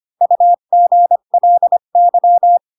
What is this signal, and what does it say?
Morse code, heard as a keyed tone, about 25 wpm, 700 Hz. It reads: UGLY
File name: Pinocchio / Pinocchio ch4 20 25 ik1hge